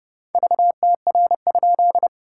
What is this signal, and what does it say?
Morse code, heard as a keyed tone, about 29 wpm, 690 Hz. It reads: VTR?
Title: SOTA QSO Practice 30wpm